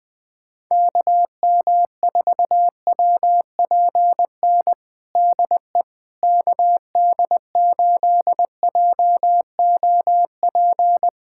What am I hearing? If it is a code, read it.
Morse code, heard as a keyed tone, about 20 wpm, 700 Hz. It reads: KM4WPN DE KD8JOP
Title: USA Callsign DE Callsign - 6 Characters - 20wpm